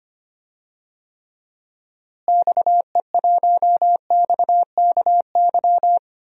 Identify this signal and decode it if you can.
Morse code, heard as a keyed tone, about 25 wpm, 700 Hz. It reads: XE1XKY
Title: DXCC-100 Call Signs - 6 Characters - Repeated 3x - 25wpm